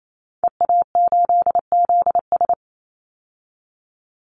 Morse code, heard as a keyed tone, about 28 wpm, 700 Hz. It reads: EA8ZS
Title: Call Signs Repeated 2x - 28wpm